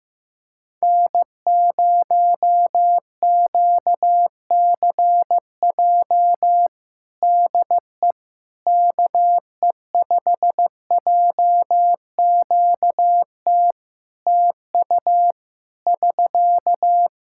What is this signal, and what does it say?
Morse code, heard as a keyed tone, about 15 wpm, 700 Hz. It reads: N0QCJ DE KE5JQT TU <SK>
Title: Practice QSO - Exchange 4 - Ending - 15wpm